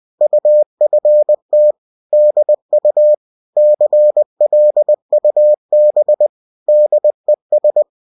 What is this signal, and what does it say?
Morse code, heard as a keyed tone, about 20 wpm, 600 Hz. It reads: UFT DU CLUB DES